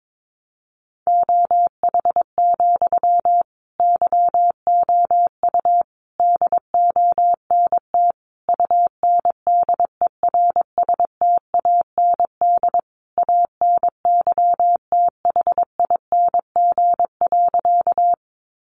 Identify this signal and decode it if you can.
Morse code, heard as a keyed tone, about 22 wpm, 700 Hz. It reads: OH, YOU DONT UNDERSTAND ANYTHING.
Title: Sentences from Top 1000 Words 22wpm